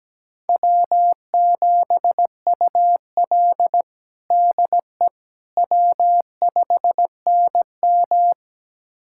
Morse code, heard as a keyed tone, about 17 wpm, 700 Hz. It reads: W7UL DE W5NM
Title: USA Callsign DE Callsign - 4 Characters - 17wpm